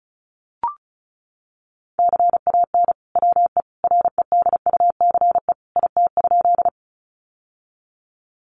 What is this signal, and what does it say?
Morse code, heard as a keyed tone, about 34 wpm, 700 Hz. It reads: CAN WE REDUCE IT?
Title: Sentences from Top 1100 Words 35wpm